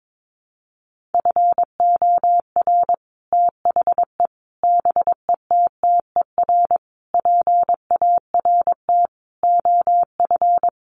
Morse code, heard as a keyed tone, about 22 wpm, 700 Hz. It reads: FOR THE BETTER PART OF
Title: Sets of 5 Words Encore2 22wpm